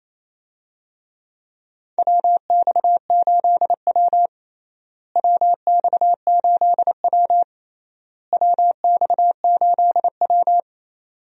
Morse code, heard as a keyed tone, about 28 wpm, 700 Hz. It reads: WX8W WX8W WX8W
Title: USA Call Signs - 4 Characters - Repeated 3x - 28wpm